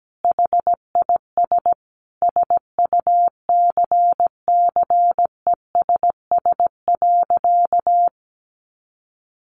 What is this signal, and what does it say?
Morse code, heard as a keyed tone, about 17 wpm, 700 Hz. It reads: HIS SUCCESS.
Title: Sentences from Top 900 Words 17wpm